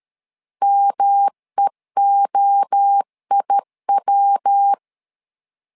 Morse code, heard as a keyed tone, about 13 wpm, 800 Hz. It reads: MEOIW
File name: LEZIONI / 4 Q1 12wpm 2028